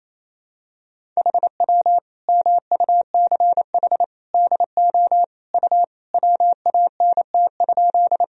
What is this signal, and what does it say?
Morse code, heard as a keyed tone, about 27 wpm, 700 Hz. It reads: HW MUCH DO U WANT?
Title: Sentences from Top 200 Words and Abbreviations 28wpm